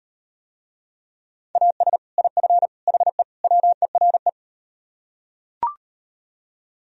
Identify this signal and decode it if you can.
Morse code, heard as a keyed tone, about 38 wpm, 700 Hz. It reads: AS IF HE WERE